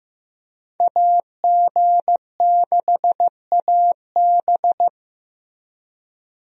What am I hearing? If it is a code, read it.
Morse code, heard as a keyed tone, about 15 wpm, 700 Hz. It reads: AG6AB
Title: USA Call Signs - 5 Characters - Repeated 2x - 15wpm